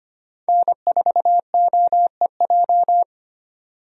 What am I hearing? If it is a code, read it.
Morse code, heard as a keyed tone, about 25 wpm, 700 Hz. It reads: N4OEJ
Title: USA Call Signs - 5 Characters - 25wpm